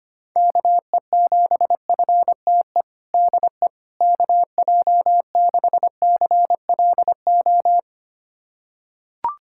Morse code, heard as a keyed tone, about 25 wpm, 700 Hz. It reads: KE7FTE DE KJ6CLO
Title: USA Callsign DE Callsign - 6 Characters - 25wpm